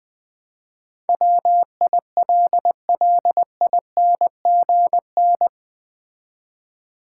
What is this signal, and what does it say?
Morse code, heard as a keyed tone, about 21 wpm, 700 Hz. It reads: WILLINGN
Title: Build A Word - 4 Letters At A Time - 11-Letter Words - 20wpm